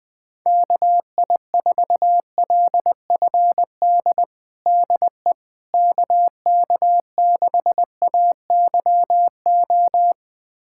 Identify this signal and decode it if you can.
Morse code, heard as a keyed tone, about 21 wpm, 700 Hz. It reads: KI4LFD DE KK6AYO
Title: USA Callsign DE Callsign 20wpm